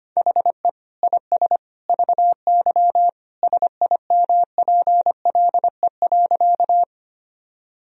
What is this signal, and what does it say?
Morse code, heard as a keyed tone, about 25 wpm, 700 Hz. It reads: HE IS VY SIMPLE.
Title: Sentences from Top 400 Words and Abbreviations 25wpm